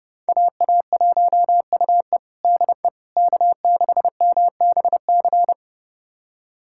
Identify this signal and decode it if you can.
Morse code, heard as a keyed tone, about 30 wpm, 700 Hz. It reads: AA1UE DE K6MBC
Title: USA Callsign DE Callsign - 5 Characters - 30wpm